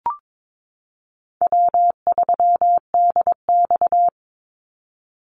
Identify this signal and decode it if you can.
Morse code, heard as a keyed tone, about 22 wpm, 700 Hz. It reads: W3DX
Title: NA SOTA Activator Callsigns 22wpm